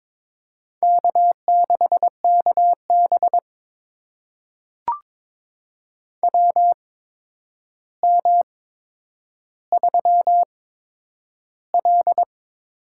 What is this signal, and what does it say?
Morse code, heard as a keyed tone, about 23 wpm, spaced as for 10 wpm, 700 Hz. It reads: K6KB WM3L
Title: USA Call Signs - 4 Characters - 8x Character Spacing - 22wpm